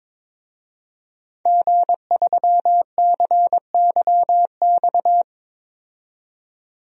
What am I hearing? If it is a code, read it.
Morse code, heard as a keyed tone, about 22 wpm, 700 Hz. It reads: G3CYX
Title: DXCC-100 Call Signs - 5 Characters - 22wpm